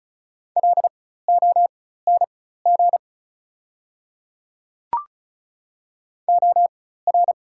Morse code, heard as a keyed tone, about 35 wpm, spaced as for 16 wpm, 700 Hz. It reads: LONG OR
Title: 4x Character Spacing - Top 100 Words - 35wpm